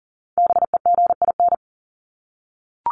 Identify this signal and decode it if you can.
Morse code, heard as a keyed tone, about 40 wpm, 700 Hz. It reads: BEGIN
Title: Top 300 Words 40wpm